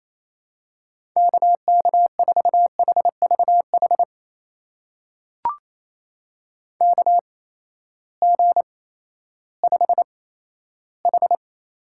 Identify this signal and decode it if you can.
Morse code, heard as a keyed tone, about 28 wpm, spaced as for 14 wpm, 700 Hz. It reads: KK4HVH KG5H
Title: USA Call Signs - 6 Characters - 8x Character Spacing - 28wpm